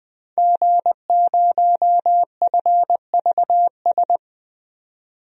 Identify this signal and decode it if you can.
Morse code, heard as a keyed tone, about 20 wpm, 700 Hz. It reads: G0FVS
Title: DXCC-100 Call Signs - 5 Characters - 20wpm